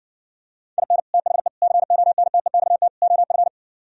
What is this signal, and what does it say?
Morse code, heard as a keyed tone, about 59 wpm, 700 Hz. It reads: IN THE CONTEXT OF